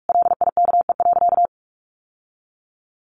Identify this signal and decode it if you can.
Morse code, heard as a keyed tone, about 44 wpm, 700 Hz. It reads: LIKE.
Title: Sentences from Top 1100 Words 45wpm